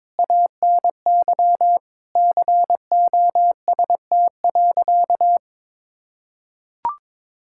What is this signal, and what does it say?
Morse code, heard as a keyed tone, about 22 wpm, 700 Hz. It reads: ANY COST.
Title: Sentences from Top 1000 Words - Review Part 5 - 22wpm